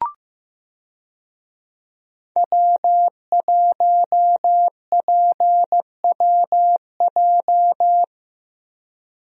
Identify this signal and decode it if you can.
Morse code, heard as a keyed tone, about 15 wpm, 700 Hz. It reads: W1PWJ
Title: USA Call Signs - 5 Characters - 15wpm